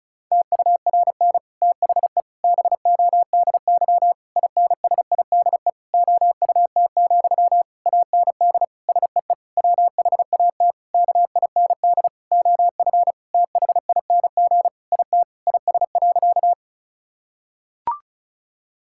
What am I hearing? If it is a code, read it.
Morse code, heard as a keyed tone, about 36 wpm, 700 Hz. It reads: TURN THE BODY INSIDE OUT, AND SEE WHAT KIND OF THING IT IS.
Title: Sentences from Top 1000 Words - Review Part 6 - 35wpm